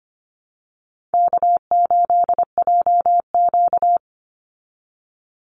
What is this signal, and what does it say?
Morse code, heard as a keyed tone, about 24 wpm, 700 Hz. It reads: K8JQ
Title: Easy Call Signs Repeated 3x - 25wpm